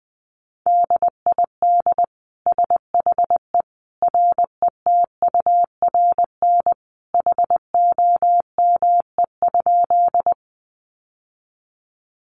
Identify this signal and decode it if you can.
Morse code, heard as a keyed tone, about 20 wpm, 700 Hz. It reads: DID SHE RETURN HOME?
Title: Sentences from Top 800 Words 20wpm